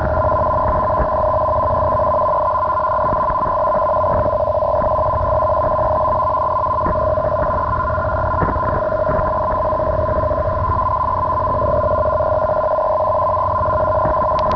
Two-tone teleprinter signal, about 400 Hz shift.
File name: g4hbt / arq-m2